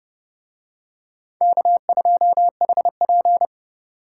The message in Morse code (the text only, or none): K2HP